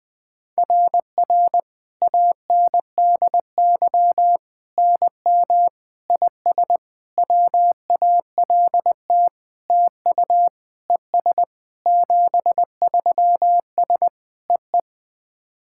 Morse code, heard as a keyed tone, about 21 wpm, 690 Hz. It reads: RR ANDY NM IS WALT TU ES 73S EE
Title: SOTA QSO Practice 20wpm